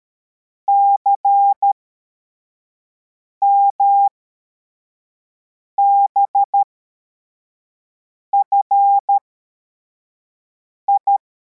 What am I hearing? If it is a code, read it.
Morse code, heard as a keyed tone, about 13 wpm, spaced as for 4 wpm, 800 Hz. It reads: CMBFI
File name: formation / cw / Text63